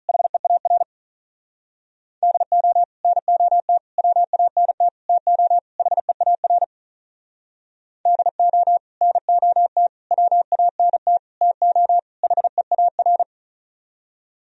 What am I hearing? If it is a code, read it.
Morse code, heard as a keyed tone, about 46 wpm, 700 Hz. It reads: HEAR DO NOT WANT TO HEAR DO NOT WANT TO HEAR